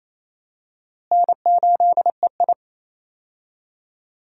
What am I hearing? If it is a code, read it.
Morse code, heard as a keyed tone, about 28 wpm, 700 Hz. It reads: N8EI
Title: USA Call Signs - 4 Characters - Repeated 2x - 28wpm